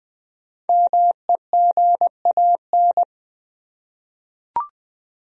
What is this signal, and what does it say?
Morse code, heard as a keyed tone, about 20 wpm, 700 Hz. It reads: MEGAN
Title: USA Common Names - Max 5 Letters - 20wpm